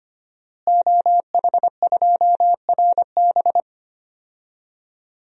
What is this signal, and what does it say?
Morse code, heard as a keyed tone, about 25 wpm, 700 Hz. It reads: OH2RB
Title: DXCC-100 Call Signs - 5 Characters - Repeated 2x - 25wpm